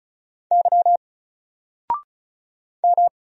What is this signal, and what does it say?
Morse code, heard as a keyed tone, about 36 wpm, 700 Hz. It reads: Y M
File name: Single-Letter-Number-35wpm